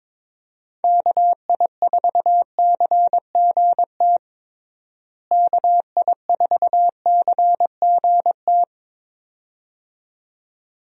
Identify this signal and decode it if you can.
Morse code, heard as a keyed tone, about 21 wpm, 700 Hz. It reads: KI4CGT KI4CGT
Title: USA Call Signs - 6 Characters - Repeated 2x - 22wpm